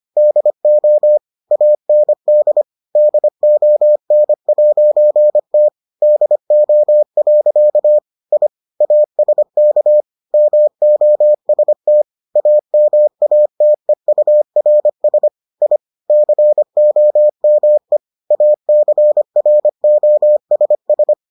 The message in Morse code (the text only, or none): DO AND DON'T DO. I ASK MOST AMATEURS I COME ACROSS